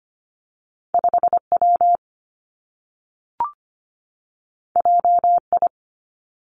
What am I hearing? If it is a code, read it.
Morse code, heard as a keyed tone, about 25 wpm, 700 Hz. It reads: 5 W J I
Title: Two Challenging Letter-Number Combinations 25wpm